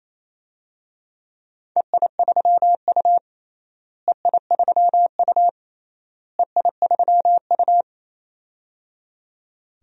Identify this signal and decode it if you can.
Morse code, heard as a keyed tone, about 28 wpm, 700 Hz. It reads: EI3U EI3U EI3U
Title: DXCC-100 Call Signs - 4 Characters - Repeated 3x - 28wpm